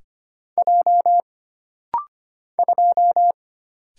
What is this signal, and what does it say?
Morse code, heard as a keyed tone, about 25 wpm, 700 Hz. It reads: J 2
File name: Single Character (Letter:Number:Punctuation:Prosign) - Rapid-Fire - 25wpm